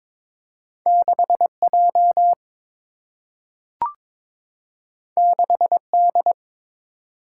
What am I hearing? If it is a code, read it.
Morse code, heard as a keyed tone, about 22 wpm, 700 Hz. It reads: 6 J 6 D
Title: Single-Letter Single-Number 22wpm